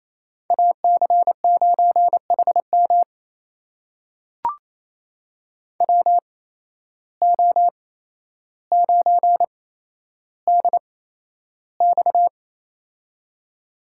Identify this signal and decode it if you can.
Morse code, heard as a keyed tone, about 28 wpm, spaced as for 12 wpm, 700 Hz. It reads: AC9HM WO9DX
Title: USA Call Signs - 5 Characters - 8x Character Spacing - 28wpm